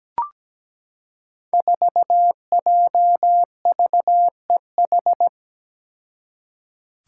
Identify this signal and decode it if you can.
Morse code, heard as a keyed tone, about 17 wpm, 700 Hz. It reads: 4JVEH
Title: Five Challenging Letter-Number Combinations 17wpm